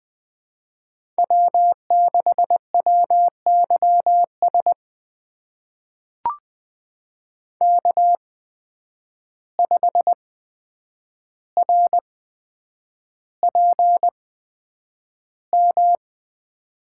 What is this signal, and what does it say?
Morse code, heard as a keyed tone, about 20 wpm, spaced as for 9 wpm, 700 Hz. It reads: W6WYS K5RPM
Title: USA Call Signs - 5 Characters - 8x Character Spacing - 20wpm